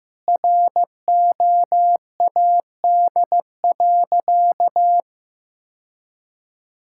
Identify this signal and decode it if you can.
Morse code, heard as a keyed tone, about 15 wpm, 700 Hz. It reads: ROAD.
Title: Sentences from Top 1600 Words 15wpm